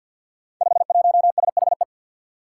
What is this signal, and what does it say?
Morse code, heard as a keyed tone, about 48 wpm, 700 Hz. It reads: 51SHE
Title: Five Challenging Letter-Number Combinations 50wpm